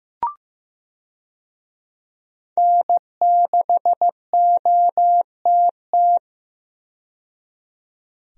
Morse code, heard as a keyed tone, about 15 wpm, 700 Hz. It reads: N6OTT
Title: USA Call Signs - 5 Characters - 15wpm